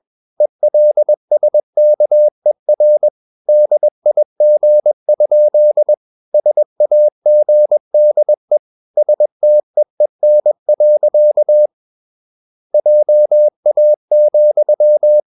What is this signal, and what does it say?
Morse code, heard as a keyed tone, about 21 wpm, 600 Hz. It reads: ELSKER DIG? SAGDE STEEN. JA,
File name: Ferie QRS 21wpm